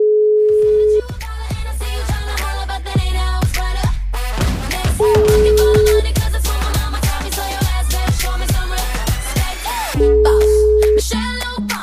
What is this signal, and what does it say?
Ringback tone (Europe, 425 Hz), with noise.